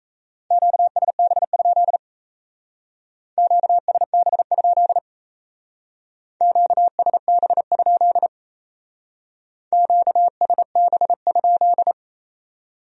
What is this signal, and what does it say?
Morse code, heard as a keyed tone, about 33 wpm, 700 Hz. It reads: QSB? QSB? QSB? QSB?
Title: Speed-Racing Q-Codes - 28wpm